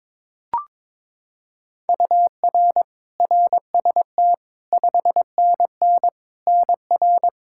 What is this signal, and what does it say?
Morse code, heard as a keyed tone, about 21 wpm, 700 Hz. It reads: UR RST 5NN NR
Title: Practice QSO - Exchange 1 - Beginning - 22wpm